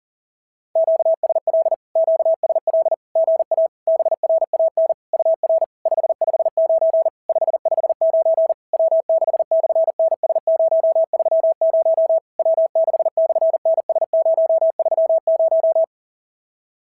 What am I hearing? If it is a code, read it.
Morse code, heard as a keyed tone, about 40 wpm, 690 Hz. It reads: QSL QSL GA BRAN UR 559 559 W6/NS030 W6/NS030
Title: SOTA QSO Practice 40wpm